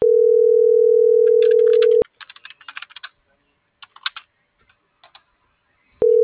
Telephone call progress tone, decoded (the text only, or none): ringback tone